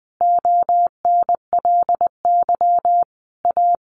Morse code, heard as a keyed tone, about 20 wpm, 700 Hz. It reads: ONLY A
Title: A Sound Of Thunder 20wpm